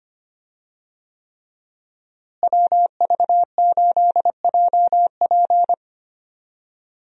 Morse code, heard as a keyed tone, about 25 wpm, 700 Hz. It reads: WV8JP
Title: USA Call Signs - 5 Characters - Repeated 3x - 25wpm